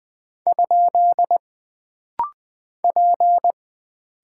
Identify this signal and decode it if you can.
Morse code, heard as a keyed tone, about 20 wpm, 700 Hz. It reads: ? P
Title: Single Character (Letter:Number:Punctuation:Prosign) - Rapid-Fire - 20wpm